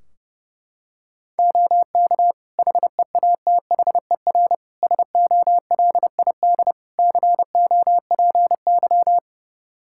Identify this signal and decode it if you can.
Morse code, heard as a keyed tone, about 30 wpm, 700 Hz. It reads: OK HEATHER SOLID COPY
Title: K1USN SST QSO Practice 30wpm